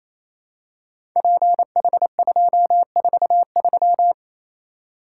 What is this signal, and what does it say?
Morse code, heard as a keyed tone, about 27 wpm, 700 Hz. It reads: PH243
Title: Code Groups - 5 Letter:Numbers - 28wpm